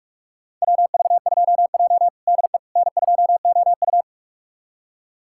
Morse code, heard as a keyed tone, about 41 wpm, 700 Hz. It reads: WV2J DE N2OU
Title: USA Callsign DE Callsign - 4 Characters - 45wpm